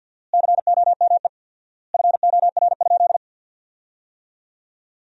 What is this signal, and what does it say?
Morse code, heard as a keyed tone, about 50 wpm, 700 Hz. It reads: COME FOR?
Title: 4x Word-Spacing - Sentences from Top 100 Words - 50wpm